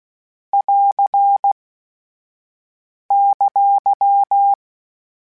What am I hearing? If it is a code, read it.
Morse code, heard as a keyed tone, about 16 wpm, 800 Hz. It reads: <AR> !
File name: PARIS Calibration / PARIS mp3s / PARIS@10-@15